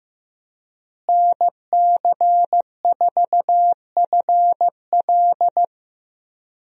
Morse code, heard as a keyed tone, about 15 wpm, 700 Hz. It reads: NC4FL
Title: USA Call Signs - 5 Characters - Repeated 2x - 15wpm